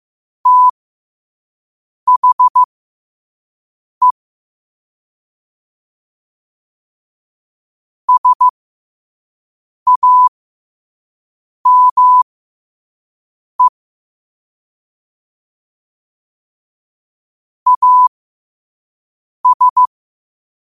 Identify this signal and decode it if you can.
Morse code, heard as a keyed tone, about 15 wpm, spaced as for 5 wpm, 1000 Hz. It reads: THE SAME AS
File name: qotdfiles / 5wpm